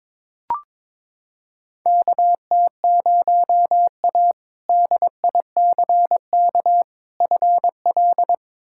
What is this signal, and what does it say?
Morse code, heard as a keyed tone, about 21 wpm, 700 Hz. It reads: KT0A DICK FL
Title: K1USN SST QSO Practice - 22wpm